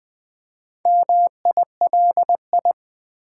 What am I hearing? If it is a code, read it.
Morse code, heard as a keyed tone, about 20 wpm, 700 Hz. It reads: MILI
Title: Build A Word - 2 Letters At A Time - 8-Letter Words - 20wpm